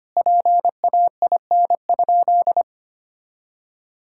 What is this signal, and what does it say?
Morse code, heard as a keyed tone, about 25 wpm, 700 Hz. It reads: PAIN?
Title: Sentences from Top 1600 Words 25wpm